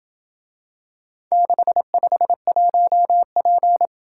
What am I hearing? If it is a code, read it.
Morse code, heard as a keyed tone, about 27 wpm, 700 Hz. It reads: 651P